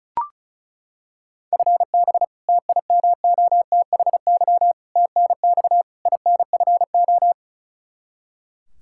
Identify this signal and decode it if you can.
Morse code, heard as a keyed tone, about 36 wpm, 700 Hz. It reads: FB TIMOTHY TNX INFO
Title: Practice QSO - Exchange 3 - More Info - 35wpm